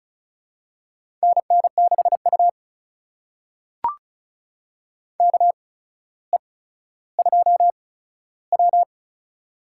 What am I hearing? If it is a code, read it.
Morse code, heard as a keyed tone, about 34 wpm, spaced as for 15 wpm, 700 Hz. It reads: NN6U KE2W
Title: USA Call Signs - 4 Characters - 8x Character Spacing - 35wpm